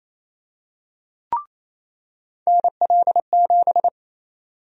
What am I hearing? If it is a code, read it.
Morse code, heard as a keyed tone, about 28 wpm, 700 Hz. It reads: NL7
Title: USA Call Sign Prefixes - 28wpm